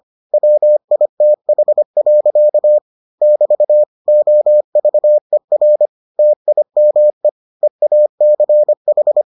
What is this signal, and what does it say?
Morse code, heard as a keyed tone, about 25 wpm, 600 Hz. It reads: WITH. <BT> OVER TIME EACH